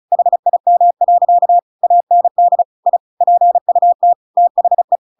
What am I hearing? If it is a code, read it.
Morse code, heard as a keyed tone, about 34 wpm, 700 Hz. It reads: HIM. AND I PUT THE
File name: Pinocchio / Pinocchio ch7 35 35 ik1hge